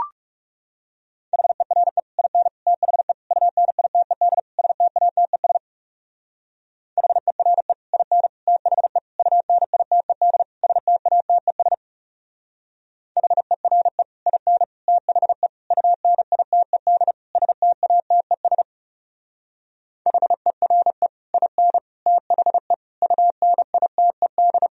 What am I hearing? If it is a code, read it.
Morse code, heard as a keyed tone, about 36 wpm, 700 Hz. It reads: HERE IN THE UNITED STATES HERE IN THE UNITED STATES HERE IN THE UNITED STATES HERE IN THE UNITED
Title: Speed-Racing - Sets of 5 Words 30wpm